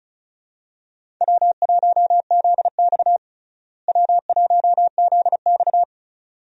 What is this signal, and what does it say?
Morse code, heard as a keyed tone, about 34 wpm, 700 Hz. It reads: W1ZX W1ZX
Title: USA Call Signs - 4 Characters - Repeated 2x - 35wpm